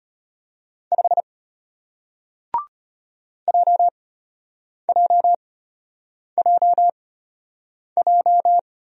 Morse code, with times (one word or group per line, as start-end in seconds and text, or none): 0.9-1.2 5
3.5-8.6 JJJJ